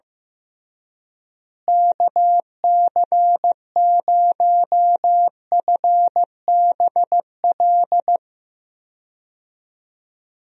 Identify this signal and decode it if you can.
Morse code, heard as a keyed tone, about 15 wpm, 700 Hz. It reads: KC0FBL
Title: USA Call Signs - 6 Characters - Repeated 2x - 15wpm